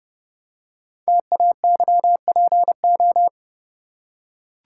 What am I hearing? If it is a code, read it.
Morse code, heard as a keyed tone, about 30 wpm, 700 Hz. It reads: TAYPO
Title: Code Groups - 5 Letters - 30wpm